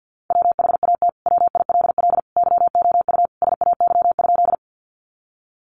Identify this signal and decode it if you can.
Morse code, heard as a keyed tone, about 50 wpm, 700 Hz. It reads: WHAT WILL YOU SAY?
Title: Sentences from Top 200 Words 50wpm